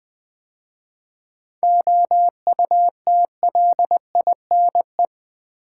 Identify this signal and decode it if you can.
Morse code, heard as a keyed tone, about 20 wpm, 700 Hz. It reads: OUTLINE
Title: 7 Letter Words - First 3 Letters Given - 20wpm